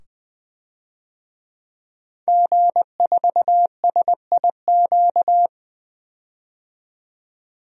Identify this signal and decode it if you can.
Morse code, heard as a keyed tone, about 20 wpm, 700 Hz. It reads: G4SIQ
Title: DXCC-100 Call Signs - 5 Characters - 20wpm